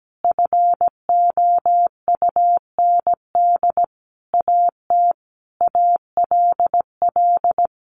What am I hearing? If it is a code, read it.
Morse code, heard as a keyed tone, about 17 wpm, 700 Hz. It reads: FOUND AT ALL
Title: Sentences from Top 1400 Words 17wpm